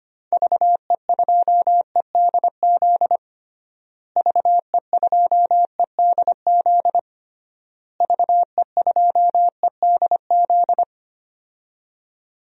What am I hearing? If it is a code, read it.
Morse code, heard as a keyed tone, about 25 wpm, 690 Hz. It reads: VE2EDZ VE2EDZ VE2EDZ
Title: DXCC-100 Call Signs - 6 Characters - Repeated 3x - 25wpm